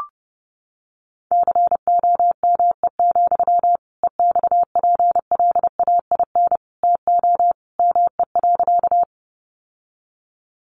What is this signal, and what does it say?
Morse code, heard as a keyed tone, about 30 wpm, 700 Hz. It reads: COME, EXPLAIN TO ME.
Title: Sentences from Top 1100 Words 30wpm